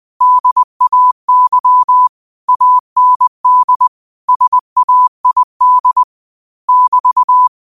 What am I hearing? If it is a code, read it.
Morse code, heard as a keyed tone, about 20 wpm, 1000 Hz. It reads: DAY AND SAID <BT>
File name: qotdfiles / 20wpm